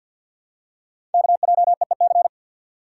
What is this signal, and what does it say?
Morse code, heard as a keyed tone, about 51 wpm, 700 Hz. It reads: KJEEC
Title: Code Groups - 5 Letters - 50wpm